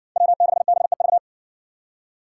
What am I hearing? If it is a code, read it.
Morse code, heard as a keyed tone, about 60 wpm, 700 Hz. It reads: W6BEV